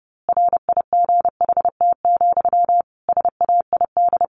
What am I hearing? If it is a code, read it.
Morse code, heard as a keyed tone, about 30 wpm, 700 Hz. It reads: RIGHT, SAID
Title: A Sound Of Thunder 30wpm